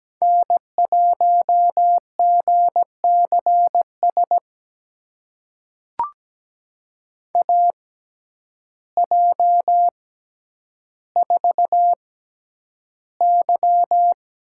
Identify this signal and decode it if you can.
Morse code, heard as a keyed tone, about 17 wpm, spaced as for 9 wpm, 700 Hz. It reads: N1GCS AJ4Y
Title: USA Call Signs - 6x Character Spacing - 17wpm